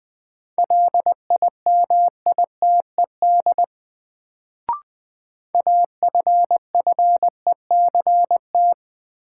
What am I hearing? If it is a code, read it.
Morse code, heard as a keyed tone, about 20 wpm, 700 Hz. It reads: LIMITED AFFECT
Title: Sentences from Top 900 Words - New Word Drill - 20wpm